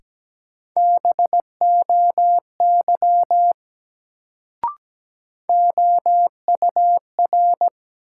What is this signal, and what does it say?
Morse code, heard as a keyed tone, about 17 wpm, 700 Hz. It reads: BOY OUR
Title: Sentences from Top 200 Words - New Word Drill - 17wpm